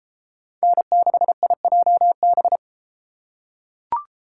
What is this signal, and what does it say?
Morse code, heard as a keyed tone, about 32 wpm, 700 Hz. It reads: N6IJB